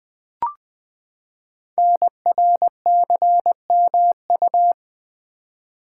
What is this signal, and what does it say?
Morse code, heard as a keyed tone, about 20 wpm, 700 Hz. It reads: NRCMU
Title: Code Groups - 5 Letters - 20wpm